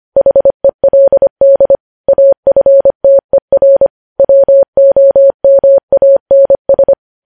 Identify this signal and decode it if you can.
Morse code, heard as a keyed tone, about 25 wpm, 550 Hz. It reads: HELD AFTER WOMANS
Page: news-headlines-in-morse-code-at-25-wpm